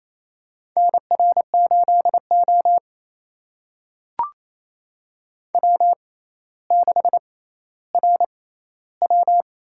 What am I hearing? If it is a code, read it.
Morse code, heard as a keyed tone, about 28 wpm, spaced as for 14 wpm, 700 Hz. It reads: NR8O W6RW
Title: USA Call Signs - 6x Character Spacing - 28wpm